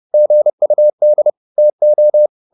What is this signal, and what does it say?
Morse code, heard as a keyed tone, about 30 wpm, 600 Hz. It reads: GUD TO